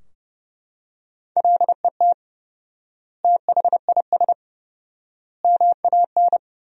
Morse code, heard as a keyed tone, about 30 wpm, 700 Hz. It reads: LET THIS MAN
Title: 4x Word-Spacing - Sentences from Top 400 Words - 30wpm